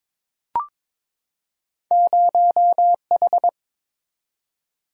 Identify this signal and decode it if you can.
Morse code, heard as a keyed tone, about 22 wpm, 700 Hz. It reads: 0H
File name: practice / 22wpm / Two Letter-Number Combinations - 22wpm